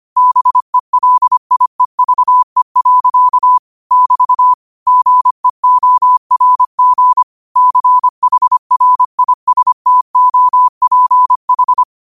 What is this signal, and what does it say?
Morse code, heard as a keyed tone, about 25 wpm, 1000 Hz. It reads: BELIEVE. <BT> GEORG CHRISTOPH
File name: qotdfiles / 25wpm